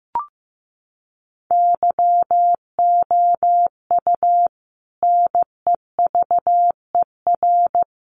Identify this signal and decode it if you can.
Morse code, heard as a keyed tone, about 15 wpm, 700 Hz. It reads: YOU NEVER
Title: Sentences from Top 1300 Words 15wpm